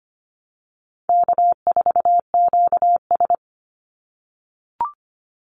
Morse code, heard as a keyed tone, about 25 wpm, 700 Hz. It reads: K4QS
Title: NA SOTA Activator Callsigns 25wpm